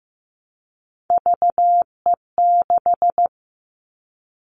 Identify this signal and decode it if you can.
Morse code, heard as a keyed tone, about 15 wpm, 700 Hz. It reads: VE6
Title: Callsign Prefixes for Top 25 DXCC - 15wpm